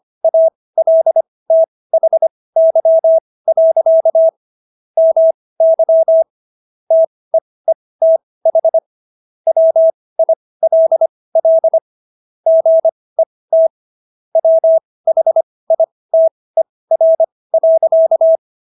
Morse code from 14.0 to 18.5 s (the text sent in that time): WHITER.